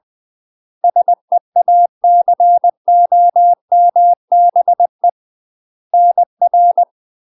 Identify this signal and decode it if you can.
Morse code, heard as a keyed tone, about 20 wpm, 700 Hz. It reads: SEACOMBE NR
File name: UK MASTER QSOs TRAINING/20WPM